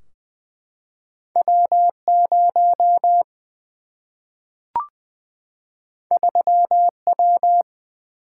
Morse code, heard as a keyed tone, about 20 wpm, 700 Hz. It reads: W0 3W
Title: Single-Letter Single-Number 20wpm